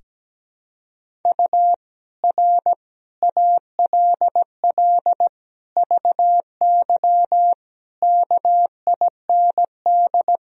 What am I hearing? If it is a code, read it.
Morse code, heard as a keyed tone, about 17 wpm, 700 Hz. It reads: U R ALL VY KIND
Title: Sentences from Top 200 Words and Abbreviations 17wpm